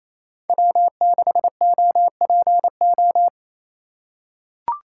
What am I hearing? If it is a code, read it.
Morse code, heard as a keyed tone, about 28 wpm, 700 Hz. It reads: W6OPO
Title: K1USN SST QSO Practice - 28wpm